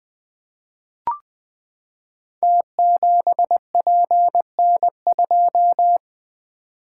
Morse code, heard as a keyed tone, about 20 wpm, 700 Hz. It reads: T7PN2
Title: Code Groups - 5 Letter:Numbers - 20wpm